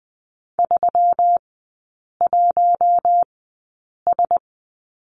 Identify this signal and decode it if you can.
Morse code, heard as a keyed tone, about 20 wpm, spaced as for 8 wpm, 700 Hz. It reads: 31S
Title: 121 Lesson 20e - Number 3 - Three Characters 20wpm